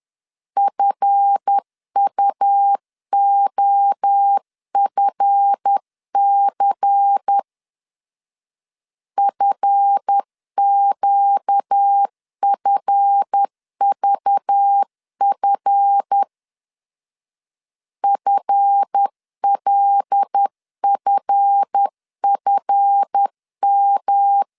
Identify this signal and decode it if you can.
Morse code, heard as a keyed tone, about 10 wpm, 800 Hz. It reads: FUOFC FQFVF FLFFM
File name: LEZIONI / 7 GF9Z / 7 GF 10wpm 2514